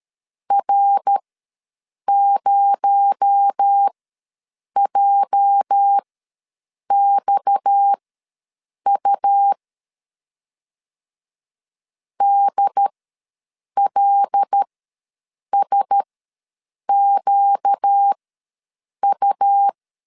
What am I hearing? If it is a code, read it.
Morse code, heard as a keyed tone, about 13 wpm, spaced as for 7 wpm, 800 Hz. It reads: R0JXU DLSQU
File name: LEZIONI / 8 YJ45 / 8 YJ ws 12wpm 3342